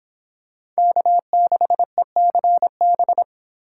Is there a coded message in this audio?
Yes: Morse code from 0.8 to 3.2 s, heard as a keyed tone, about 26 wpm, 700 Hz. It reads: K6ECB